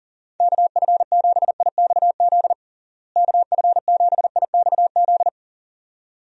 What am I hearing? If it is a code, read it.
Morse code, heard as a keyed tone, about 37 wpm, 700 Hz. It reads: KF7IXZ KF7IXZ
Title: USA Call Signs - 6 Characters - Repeated 2x - 40wpm